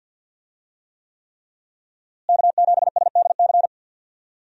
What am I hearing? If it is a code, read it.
Morse code, heard as a keyed tone, about 52 wpm, 700 Hz. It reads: K7SDC
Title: USA Call Signs - 5 Characters - Repeated 3x - 50wpm